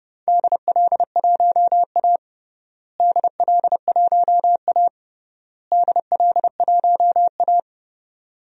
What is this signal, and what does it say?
Morse code, heard as a keyed tone, about 30 wpm, 700 Hz. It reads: DL1A DL1A DL1A
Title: DXCC-100 Call Signs - 4 Characters - Repeated 3x - 30wpm